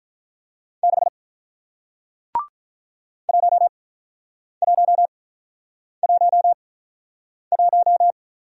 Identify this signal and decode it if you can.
Morse code, heard as a keyed tone, about 47 wpm, spaced as for 9 wpm, 700 Hz. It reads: 6 1111